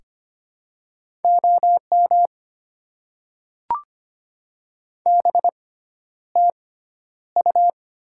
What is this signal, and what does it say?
Morse code, heard as a keyed tone, about 25 wpm, spaced as for 11 wpm, 700 Hz. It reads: OM BTU
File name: Top 90 QSO Elements - 6x Character Spacing - 25wpm